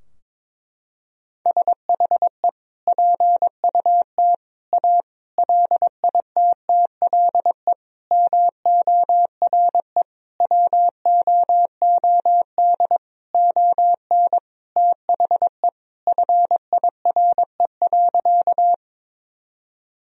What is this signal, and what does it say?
Morse code, heard as a keyed tone, about 22 wpm, 700 Hz. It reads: SHE PUT A LITTLE MORE WOOD ON THE FIRE.
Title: Sentences from Top 1000 Words - Review Part 2 - 22wpm